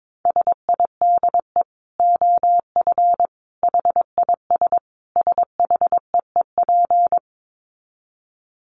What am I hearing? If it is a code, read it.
Morse code, heard as a keyed tone, about 22 wpm, 700 Hz. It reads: SIDE OF HIS SHEEP
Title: Dit Challenge 22wpm